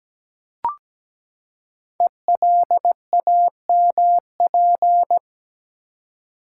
Morse code, heard as a keyed tone, about 17 wpm, 700 Hz. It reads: ELAMP
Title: Code Groups - 5 Letters - 17wpm